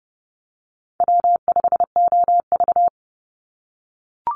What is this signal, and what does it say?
Morse code, heard as a keyed tone, about 30 wpm, 700 Hz. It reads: W5OV
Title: Call Signs - 30wpm